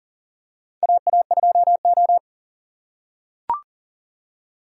Morse code, heard as a keyed tone, about 40 wpm, 700 Hz. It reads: AA2O
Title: USA Call Signs - 4x Character Spacing - 40wpm